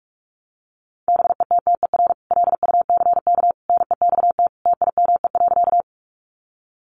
Morse code, heard as a keyed tone, about 44 wpm, 700 Hz. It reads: BETTER LUCK NEXT TIME.
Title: Sentences from Top 1500 Words 45wpm